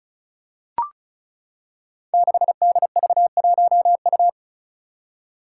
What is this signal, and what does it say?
Morse code, heard as a keyed tone, about 36 wpm, 700 Hz. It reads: 6DV1U
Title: Five Challenging Letter-Number Combinations 35wpm